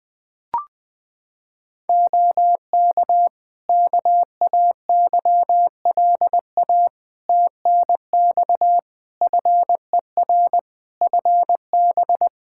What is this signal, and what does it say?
Morse code, heard as a keyed tone, about 20 wpm, 700 Hz. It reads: OK KAYLA TNX FER FB
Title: Practice QSO - Exchange 4 - Ending - 20wpm